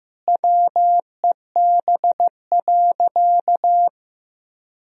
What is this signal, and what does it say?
Morse code, heard as a keyed tone, about 15 wpm, 700 Hz. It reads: WEB.
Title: Sentences from Top 1000 Words - Review Part 4 - 15wpm